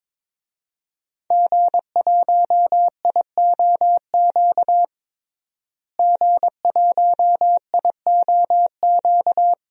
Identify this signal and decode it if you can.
Morse code, heard as a keyed tone, about 21 wpm, 700 Hz. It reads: G1IOQ G1IOQ
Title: DXCC-100 Call Signs - 5 Characters - Repeated 3x - 22wpm